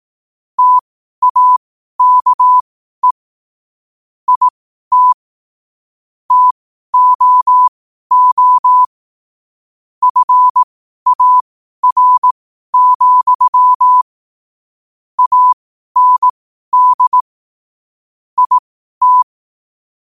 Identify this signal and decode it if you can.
Morse code, heard as a keyed tone, about 18 wpm, spaced as for 12 wpm, 1000 Hz. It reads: TAKE IT TOO FAR, AND IT
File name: qotdfiles / 12wpm